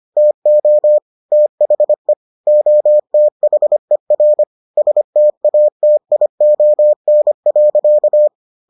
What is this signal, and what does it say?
Morse code, heard as a keyed tone, about 25 wpm, 600 Hz. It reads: TO THE OTHER STATION.